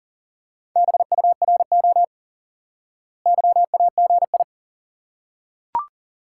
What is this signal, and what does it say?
Morse code, heard as a keyed tone, about 40 wpm, 700 Hz. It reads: BURO YAGI
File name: IQE Course 3 - Lesson 89 - YAGI - Two Elements - 40wpm